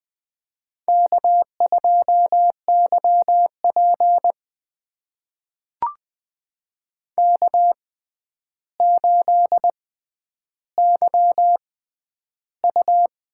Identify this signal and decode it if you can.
Morse code, heard as a keyed tone, about 20 wpm, spaced as for 10 wpm, 700 Hz. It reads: K2YP K8YU
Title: USA Call Signs - 4 Characters - 6x Character Spacing - 20wpm